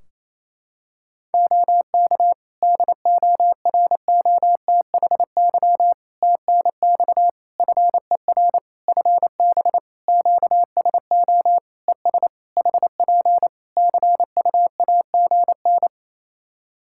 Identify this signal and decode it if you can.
Morse code, heard as a keyed tone, about 27 wpm, 700 Hz. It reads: OK DOROTHY TNX FER FB QSO ES HP CUAGN
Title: Practice QSO - Exchange 4 - Ending - 28wpm